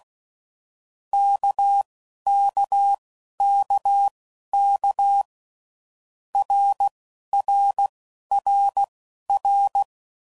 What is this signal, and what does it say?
Morse code, heard as a keyed tone, about 16 wpm, spaced as for 11 wpm, 800 Hz. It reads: KKKK RRRR
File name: Groups / Basic Letter Groups / RKPX